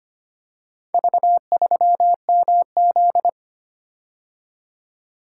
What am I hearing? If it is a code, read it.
Morse code, heard as a keyed tone, about 25 wpm, 700 Hz. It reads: V3MZ
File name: practice / 25wpm / Four Letter-Number Combinations - 25wpm